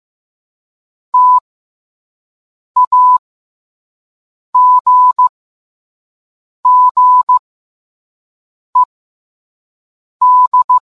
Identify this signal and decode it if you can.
Morse code, heard as a keyed tone, about 15 wpm, spaced as for 5 wpm, 1000 Hz. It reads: TAGGED